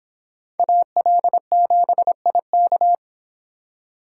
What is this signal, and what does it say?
Morse code, heard as a keyed tone, about 25 wpm, 700 Hz. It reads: AL7IK